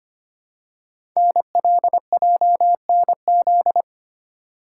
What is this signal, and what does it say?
Morse code, heard as a keyed tone, about 25 wpm, 700 Hz. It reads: NLJNZ
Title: Code Groups - 5 Letters - 25wpm